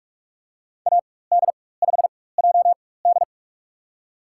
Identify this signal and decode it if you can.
Morse code, heard as a keyed tone, about 46 wpm, spaced as for 21 wpm, 700 Hz. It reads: AD5JD
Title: USA Call Signs - 4x Character Spacing - 45wpm